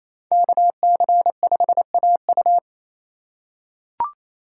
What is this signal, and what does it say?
Morse code, heard as a keyed tone, about 28 wpm, 700 Hz. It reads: KC5AU
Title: USA Call Signs - 5 Characters - 4x Character Spacing - 28wpm